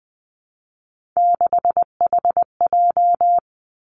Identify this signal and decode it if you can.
Morse code, heard as a keyed tone, about 20 wpm, 700 Hz. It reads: 6HJ
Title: Three Challenging Letter-Number Combinations 20wpm